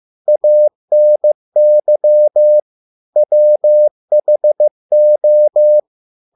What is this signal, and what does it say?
Morse code, heard as a keyed tone, about 15 wpm, 600 Hz. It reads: ANY WHO